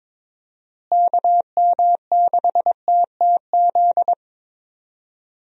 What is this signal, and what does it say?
Morse code, heard as a keyed tone, about 21 wpm, 700 Hz. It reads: KM6TTZ
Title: USA Call Signs - 6 Characters - Repeated 2x - 22wpm